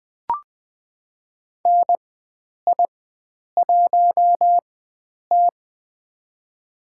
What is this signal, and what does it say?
Morse code, heard as a keyed tone, about 20 wpm, spaced as for 9 wpm, 700 Hz. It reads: NI1T
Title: USA Call Signs - 4 Characters - 4x Character Spacing - 20wpm